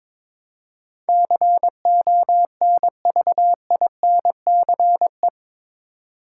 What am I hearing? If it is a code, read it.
Morse code, heard as a keyed tone, about 21 wpm, 700 Hz. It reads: CONVINCE
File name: practice / 22wpm / Eight Letter Words - 22wpm